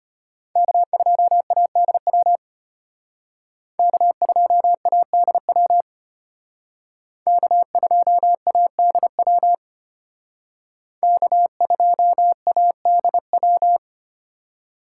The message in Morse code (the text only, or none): K2ADW K2ADW K2ADW K2ADW